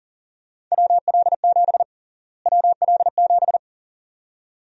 Morse code, heard as a keyed tone, about 37 wpm, 700 Hz. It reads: WL7 WL7
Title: USA Call Sign Prefixes - Repeated 2x - 40wpm